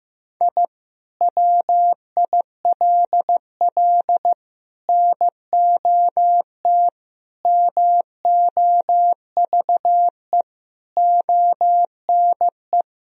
Morse code, heard as a keyed tone, about 15 wpm, 700 Hz. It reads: I WILL NOT MOVE ONE
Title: Sentences from Top 1000 Words - Review Part 3 - 15wpm